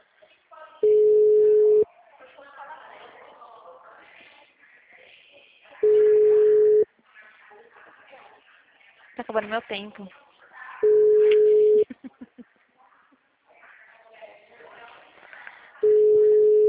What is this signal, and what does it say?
Ringback tone (Europe, 425 Hz), with speech.